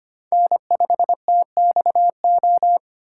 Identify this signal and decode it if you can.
Morse code, heard as a keyed tone, about 25 wpm, 700 Hz. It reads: N5TXO
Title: USA Call Signs - 5 Characters - 2x Character Spacing - 25wpm